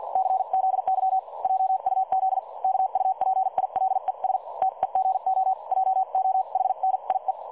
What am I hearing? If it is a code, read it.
Morse code, heard as a keyed tone, about 50 wpm, spaced as for 43 wpm, 760 Hz. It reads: CZY JUZ KUPILES TEGO JOHNIE
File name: dlazaawansowanych / nagraniazpasma / websdr recording 3565.0kHz